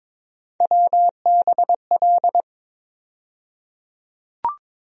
Morse code, heard as a keyed tone, about 22 wpm, 700 Hz. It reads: WBL
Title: USA Call Sign Postfixes - 3 Letters 22wpm